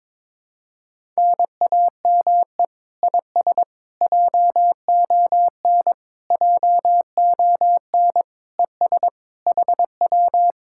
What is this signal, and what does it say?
Morse code, heard as a keyed tone, about 22 wpm, 700 Hz. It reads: NAME IS JON JON ES HW
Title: Speed-Racing Common QSO Phrases - 22wpm